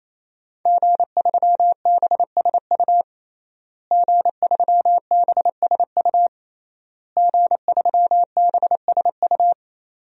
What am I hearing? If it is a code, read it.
Morse code, heard as a keyed tone, about 28 wpm, 700 Hz. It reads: G3BSU G3BSU G3BSU
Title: DXCC-100 Call Signs - 5 Characters - Repeated 3x - 28wpm